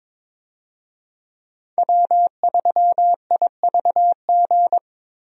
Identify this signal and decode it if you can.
Morse code, heard as a keyed tone, about 22 wpm, 700 Hz. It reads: W3IVG
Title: USA Call Signs - 5 Characters - 22wpm